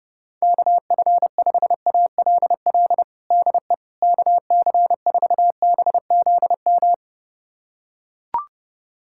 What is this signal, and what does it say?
Morse code, heard as a keyed tone, about 30 wpm, 700 Hz. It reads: KF5ALL DE KC4BZM
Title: USA Callsign DE Callsign - 6 Characters - 30wpm